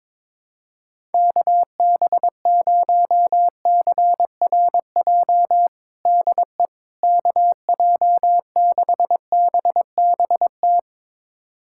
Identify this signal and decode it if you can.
Morse code, heard as a keyed tone, about 22 wpm, 700 Hz. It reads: KB0CRJ DE KJ6BBT
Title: USA Callsign DE Callsign - 6 Characters - 22wpm